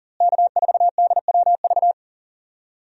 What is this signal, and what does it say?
Morse code, heard as a keyed tone, about 40 wpm, 700 Hz. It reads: K4DWV
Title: USA Call Signs - 5 Characters - Repeated 2x - 40wpm